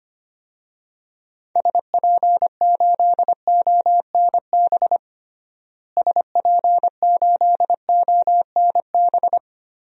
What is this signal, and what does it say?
Morse code, heard as a keyed tone, about 25 wpm, 700 Hz. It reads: SP8ONB SP8ONB
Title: DXCC-100 Call Signs - 6 Characters - Repeated 3x - 25wpm